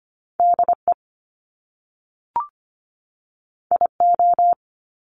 Morse code, heard as a keyed tone, about 25 wpm, 700 Hz. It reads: DE IO
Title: Top Two Letter Combinations in Words 25wpm